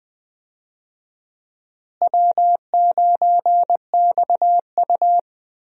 Morse code, heard as a keyed tone, about 20 wpm, 700 Hz. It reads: W9XU
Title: USA Call Signs - 4 Characters - Repeated 2x - 20wpm